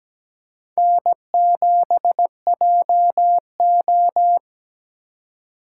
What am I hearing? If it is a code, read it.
Morse code, heard as a keyed tone, about 17 wpm, 700 Hz. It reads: N7JO
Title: USA Call Signs - 4 Characters - 2x Character Spacing - 17wpm